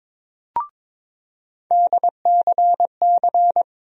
Morse code, heard as a keyed tone, about 22 wpm, 700 Hz. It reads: DCC
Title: USA Call Sign Postfixes - 3 Letters 22wpm